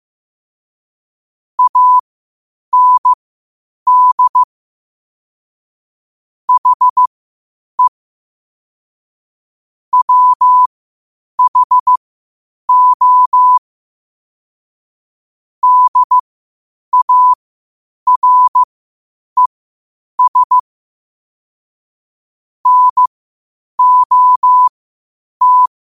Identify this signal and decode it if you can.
Morse code, heard as a keyed tone, about 15 wpm, spaced as for 8 wpm, 1000 Hz. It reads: AND HE WHO DARES NOT
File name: qotdfiles / 8wpm